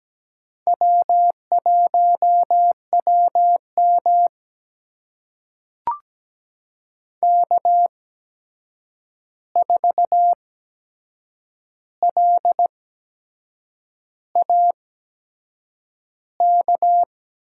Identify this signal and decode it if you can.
Morse code, heard as a keyed tone, about 17 wpm, spaced as for 7 wpm, 700 Hz. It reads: W1WM K4LAK
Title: USA Call Signs - 8x Character Spacing - 17wpm